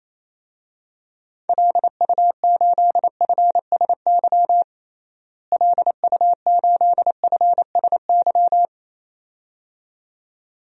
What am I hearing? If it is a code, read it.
Morse code, heard as a keyed tone, about 28 wpm, 700 Hz. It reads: LU8FSY LU8FSY
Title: DXCC-100 Call Signs - 6 Characters - Repeated 2x - 28wpm